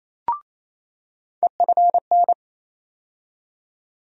Morse code, heard as a keyed tone, about 28 wpm, 700 Hz. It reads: EFN
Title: USA Call Sign Postfixes - 3 Letters 28wpm